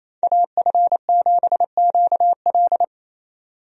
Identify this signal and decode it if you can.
Morse code, heard as a keyed tone, about 28 wpm, 700 Hz. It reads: AF7QL
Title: USA Call Signs - 5 Characters - 28wpm